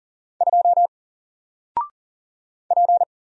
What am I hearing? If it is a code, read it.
Morse code, heard as a keyed tone, about 40 wpm, 700 Hz. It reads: 2 P
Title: Single Letter-Number Alt 40wpm